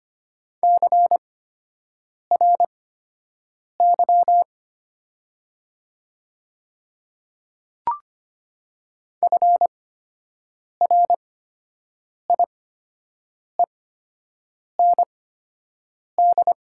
Morse code, heard as a keyed tone, about 25 wpm, spaced as for 7 wpm, 700 Hz. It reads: CRY FRIEND
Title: 8x Character Spacing - Top 500 Words - 25wpm